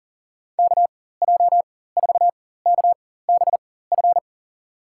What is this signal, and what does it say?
Morse code, heard as a keyed tone, about 37 wpm, spaced as for 18 wpm, 700 Hz. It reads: KJ4KBF
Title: USA Call Signs - 4x Character Spacing - 40wpm